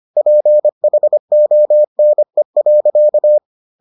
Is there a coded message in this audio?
Yes: Morse code from 0.2 to 3.4 s, heard as a keyed tone, about 25 wpm, 600 Hz. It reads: PHONE.